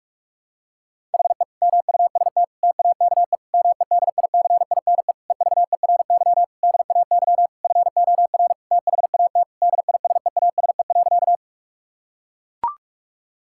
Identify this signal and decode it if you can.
Morse code, heard as a keyed tone, about 45 wpm, 700 Hz. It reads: HE MUST TAKE MEDICINE EVERY DAY FOR THAT DISEASE.
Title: Sentences from Top 1000 Words - Review Part 6 - 45wpm